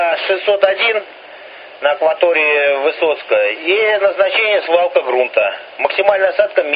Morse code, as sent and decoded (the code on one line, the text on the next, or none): none